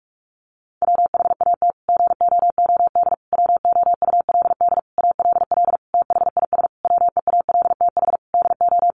Morse code, heard as a keyed tone, about 44 wpm, 700 Hz. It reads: WHAT GOOD WOULD ALL THIS WEALTH DO
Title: Sentences from Top 1200 Words 45wpm